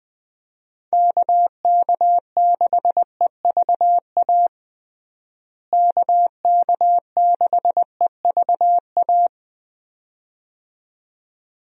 Morse code, heard as a keyed tone, about 20 wpm, 700 Hz. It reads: KK6EVA KK6EVA
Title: USA Call Signs - 6 Characters - Repeated 2x - 20wpm